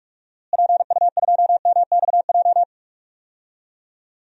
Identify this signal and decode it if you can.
Morse code, heard as a keyed tone, about 44 wpm, 700 Hz. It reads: PU2MXJ
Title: DXCC-100 Call Signs - 6 Characters - Repeated 2x - 45wpm